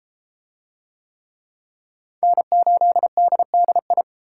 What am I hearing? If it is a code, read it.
Morse code, heard as a keyed tone, about 32 wpm, 700 Hz. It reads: N8DDI